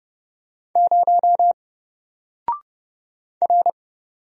Morse code, heard as a keyed tone, about 30 wpm, 700 Hz. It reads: 0 R
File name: Single-Letter-Number-30wpm